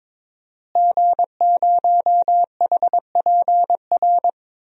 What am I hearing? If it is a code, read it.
Morse code, heard as a keyed tone, about 22 wpm, 700 Hz. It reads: G0HPR
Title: DXCC-100 Call Signs - 5 Characters - Repeated 3x - 22wpm